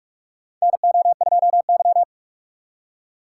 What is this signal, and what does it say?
Morse code, heard as a keyed tone, about 46 wpm, 690 Hz. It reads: NO2Y
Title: USA Call Signs - 4 Characters - 3x Character Spacing - 45wpm